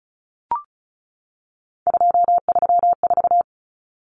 Morse code, heard as a keyed tone, about 34 wpm, 700 Hz. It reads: 234
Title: Numbers - 3-Digit 35wpm